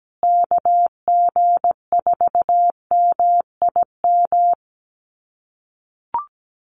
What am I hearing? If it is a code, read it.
Morse code, heard as a keyed tone, about 17 wpm, 700 Hz. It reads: KG4MIM
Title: USA Call Signs Repeated 3x 17wpm